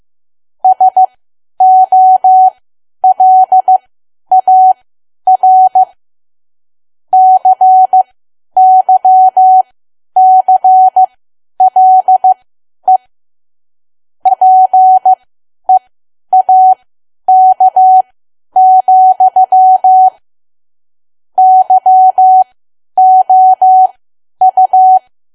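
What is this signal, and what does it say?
Morse code, heard as a keyed tone, about 15 wpm, spaced as for 10 wpm, 750 Hz. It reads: SOLAR CYCLE PEAK, YOU
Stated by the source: Morse learning practice at 10 wpm.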